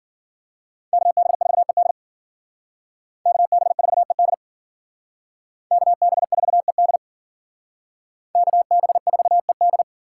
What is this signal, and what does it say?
Morse code, heard as a keyed tone, about 52 wpm, 700 Hz. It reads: KB4ED KB4ED KB4ED KB4ED